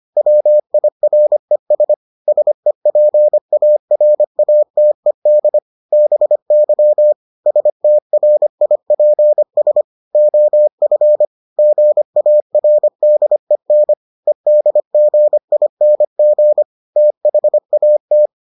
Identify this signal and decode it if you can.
Morse code, heard as a keyed tone, about 25 wpm, 600 Hz. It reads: WIRES SEPARATED BY STRIPS OF GARDEN EDGING THAT